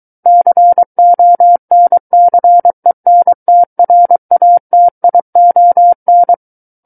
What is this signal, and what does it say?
Morse code, heard as a keyed tone, about 23 wpm, 700 Hz. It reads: CONCENTRATION
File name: Book One / 14-Chapter 14 - 23 WPM